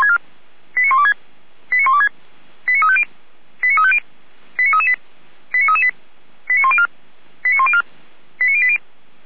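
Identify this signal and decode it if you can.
Five-tone selective call (ZVEI), heard as quick runs of five tones, tones of about 70 ms. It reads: 89126 89126 89350 89350 89338 89338 89104 89104 89080
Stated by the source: Five-tone ZVEI selective call 89235 received by SDR, server of Feuerwehr Orlamünde.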